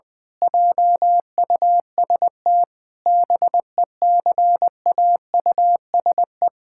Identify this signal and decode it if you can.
Morse code, heard as a keyed tone, about 20 wpm, 700 Hz. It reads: JUST BECAUSE
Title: Pangram Challenge v2 - 20wpm